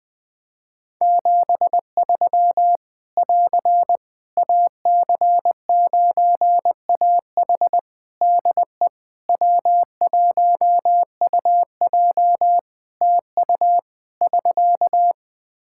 Morse code, heard as a keyed tone, about 20 wpm, 700 Hz. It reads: 73 <AR> AC9AH DE W1UJ TU <SK>
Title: K1USN SST QSO Practice 20wpm